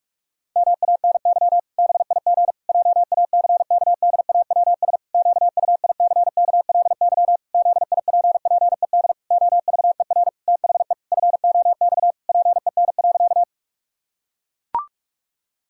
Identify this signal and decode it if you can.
Morse code, heard as a keyed tone, about 41 wpm, 700 Hz. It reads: MANY BIG JACKDAWS QUICKLY ZIPPED OVER THE FOX PEN.
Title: Pangram Challenge v2 - 45wpm